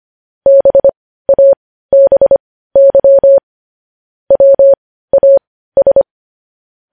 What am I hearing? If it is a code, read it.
Morse code, heard as a keyed tone, about 25 wpm, spaced as for 15 wpm, 550 Hz. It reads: BABY WAS